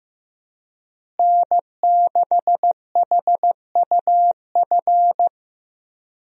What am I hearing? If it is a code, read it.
Morse code, heard as a keyed tone, about 15 wpm, 700 Hz. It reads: N6HUF
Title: USA Call Signs - 5 Characters - Repeated 2x - 15wpm